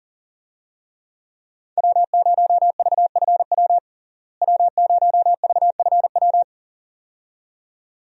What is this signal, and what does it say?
Morse code, heard as a keyed tone, about 40 wpm, 700 Hz. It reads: W0VFW W0VFW
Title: USA Call Signs - 5 Characters - Repeated 2x - 40wpm